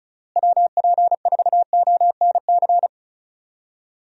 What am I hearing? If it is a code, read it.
Morse code, heard as a keyed tone, about 34 wpm, 700 Hz. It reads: WP4ONC
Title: USA Call Signs - 6 Characters - Repeated 3x - 35wpm